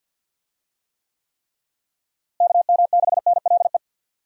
Morse code, heard as a keyed tone, about 52 wpm, 700 Hz. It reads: KM6NLE
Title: USA Call Signs - 6 Characters - Repeated 3x - 50wpm